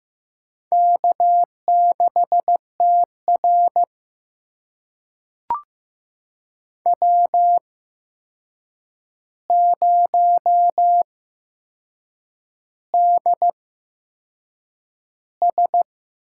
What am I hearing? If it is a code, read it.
Morse code, heard as a keyed tone, about 15 wpm, spaced as for 6 wpm, 700 Hz. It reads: K6TR W0DS
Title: USA Call Signs - 4 Characters - 8x Character Spacing - 15wpm